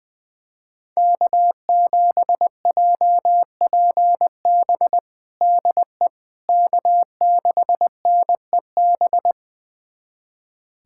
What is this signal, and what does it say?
Morse code, heard as a keyed tone, about 21 wpm, 700 Hz. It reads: K7JPB DE K6NEB
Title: USA Callsign DE Callsign 20wpm